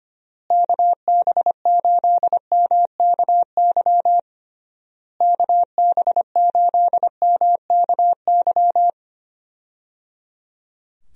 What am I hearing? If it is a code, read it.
Morse code, heard as a keyed tone, about 25 wpm, 700 Hz. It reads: KB8MKY KB8MKY
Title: USA Call Signs - 6 Characters - Repeated 3x - 25wpm